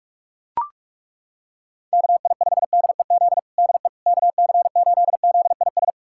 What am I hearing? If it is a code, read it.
Morse code, heard as a keyed tone, about 41 wpm, 700 Hz. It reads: KI5DEZ DE KC8ZIS
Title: USA Callsign DE Callsign 45wpm